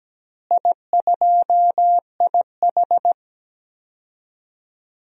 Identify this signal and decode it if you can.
Morse code, heard as a keyed tone, about 17 wpm, 700 Hz. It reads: I2IH
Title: DXCC-100 Call Signs - 4 Characters - Repeated 2x - 17wpm